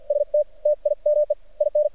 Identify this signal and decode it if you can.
Morse code, heard as a keyed tone, about 53 wpm, spaced as for 45 wpm, 600 Hz. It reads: ST TIME IN
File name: archived / CRSnet